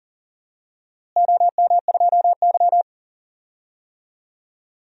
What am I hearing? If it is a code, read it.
Morse code, heard as a keyed tone, about 40 wpm, 700 Hz. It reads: OM2Y
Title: DXCC-100 Call Signs - 4 Characters - Repeated 2x - 40wpm